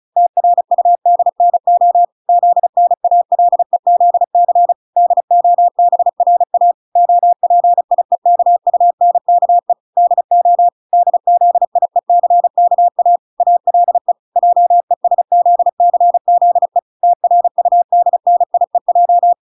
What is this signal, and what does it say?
Morse code, heard as a keyed tone, about 34 wpm, 700 Hz. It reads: TRUDNO ZNALEZC DOBRA OPIEKUNKE DO DZIECKA ALE JESZCZE TRUDNIEJ